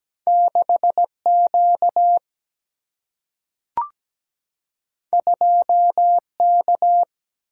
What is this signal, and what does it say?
Morse code, heard as a keyed tone, about 17 wpm, 700 Hz. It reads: 6 Q 2 K
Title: Single-Letter Single-Number 17wpm